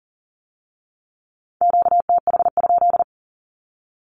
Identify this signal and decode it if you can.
Morse code, heard as a keyed tone, about 43 wpm, 700 Hz. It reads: QTH?
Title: Q-Codes with Spoken Meaning - 40wpm